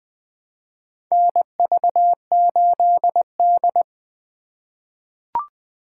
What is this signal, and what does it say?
Morse code, heard as a keyed tone, about 21 wpm, 700 Hz. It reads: NV8D
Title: USA Call Signs - 2x Character Spacing - 20wpm